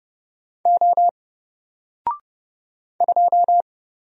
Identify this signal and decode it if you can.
Morse code, heard as a keyed tone, about 30 wpm, 700 Hz. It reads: O 2
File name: Single-Letter-Number-30wpm